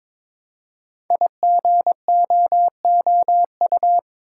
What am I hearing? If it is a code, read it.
Morse code, heard as a keyed tone, about 22 wpm, 700 Hz. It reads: IGOOU
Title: Code Groups - 5 Letters - 22wpm